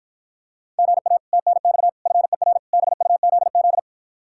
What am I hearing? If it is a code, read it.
Morse code, heard as a keyed tone, about 54 wpm, 700 Hz. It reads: GA TNX FER BUZZ